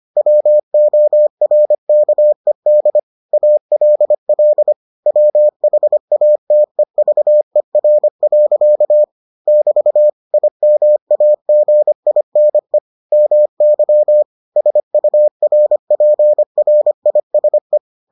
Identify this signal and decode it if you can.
Morse code, heard as a keyed tone, about 25 wpm, 600 Hz. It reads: WORKED ALL WHATEVER. <BT> IMAGINE MY SURPRISE